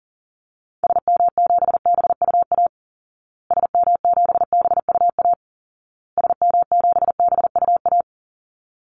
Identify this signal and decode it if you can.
Morse code, heard as a keyed tone, about 40 wpm, 700 Hz. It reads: SM7BUA SM7BUA SM7BUA
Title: Call Signs Repeated 2x - 40wpm